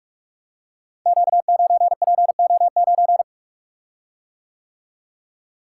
Morse code, heard as a keyed tone, about 44 wpm, 700 Hz. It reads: Q9PO9
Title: Code Groups - 5 Letter:Numbers - 45wpm